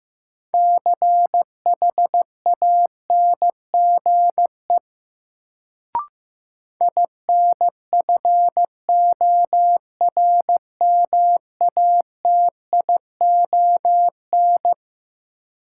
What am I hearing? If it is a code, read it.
Morse code, heard as a keyed tone, about 15 wpm, 700 Hz. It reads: CHANGE INFORMATION
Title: Sentences from Top 600 Words - New Word Drill - 15wpm